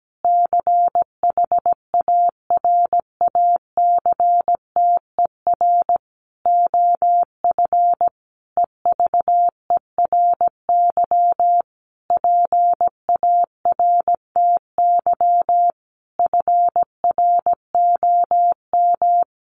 Morse code, heard as a keyed tone, about 17 wpm, 700 Hz. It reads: CHARACTER OF EVERY PARTY FROM
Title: Sentences from Top 1400 Words 17wpm